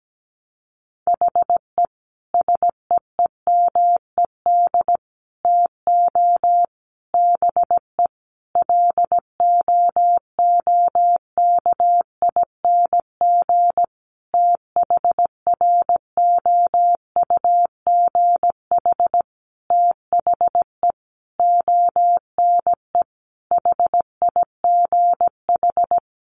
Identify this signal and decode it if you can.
Morse code, heard as a keyed tone, about 17 wpm, 700 Hz. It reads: HE SEEMED TO BE LOOKING THROUGH THE ONE HIGH
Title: A Sound Of Thunder 17wpm